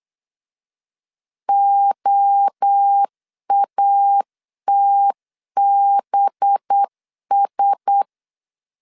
Morse code, heard as a keyed tone, about 8 wpm, 800 Hz. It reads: OATBS
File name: CW / DET / 1 6 1500C S C 8wpm 5219